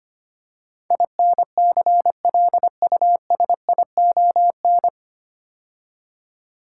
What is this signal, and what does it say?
Morse code, heard as a keyed tone, about 25 wpm, 700 Hz. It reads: INCLUSION
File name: practice / 25wpm / 9 Letter Words - First 3 Letters Given - 25wpm